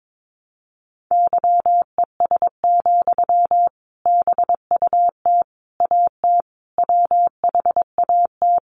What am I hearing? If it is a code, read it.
Morse code, heard as a keyed tone, about 22 wpm, 700 Hz. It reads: YES, BUT AT WHAT
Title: Sentences from Top 800 Words 22wpm